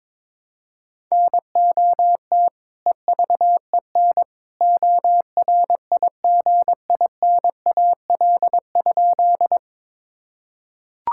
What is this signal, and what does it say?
Morse code, heard as a keyed tone, about 22 wpm, 700 Hz. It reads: NOT EVEN ORIGINAL?
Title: Sentences from Top 1000 Words - Review Part 6 - 22wpm